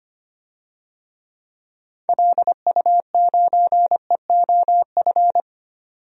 Morse code, heard as a keyed tone, about 25 wpm, 700 Hz. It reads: LU9EOF
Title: DXCC-100 Call Signs - 6 Characters - 25wpm